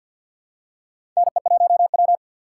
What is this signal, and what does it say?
Morse code, heard as a keyed tone, about 50 wpm, 700 Hz. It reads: NE1W
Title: USA Call Signs - 4 Characters - 3x Character Spacing - 50wpm